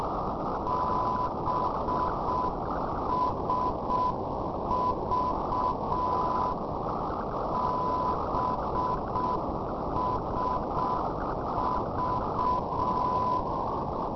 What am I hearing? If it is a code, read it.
Morse code, heard as a keyed tone, about 6 wpm, 1000 Hz, with noise. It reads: BSV BSV
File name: europe / rou / rou-324 ndb boboc bsv-397-2020-yo3ksr